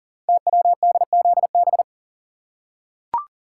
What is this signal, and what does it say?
Morse code, heard as a keyed tone, about 42 wpm, 690 Hz. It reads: TWDZB
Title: Code Groups - 5 Letters - 40wpm